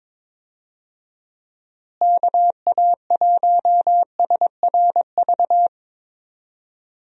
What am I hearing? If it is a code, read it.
Morse code, heard as a keyed tone, about 22 wpm, 700 Hz. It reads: KA1SRV
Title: USA Call Signs - 6 Characters - Repeated 3x - 22wpm